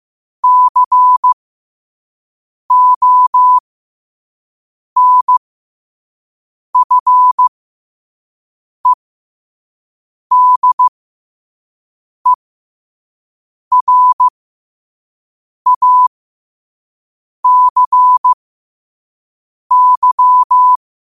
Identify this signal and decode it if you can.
Morse code, heard as a keyed tone, about 15 wpm, spaced as for 5 wpm, 1000 Hz. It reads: CONFEDERACY
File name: qotdfiles / 5wpm